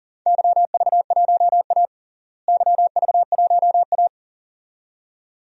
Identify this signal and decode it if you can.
Morse code, heard as a keyed tone, about 40 wpm, 700 Hz. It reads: YV1A YV1A
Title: DXCC-100 Call Signs - 4 Characters - Repeated 2x - 40wpm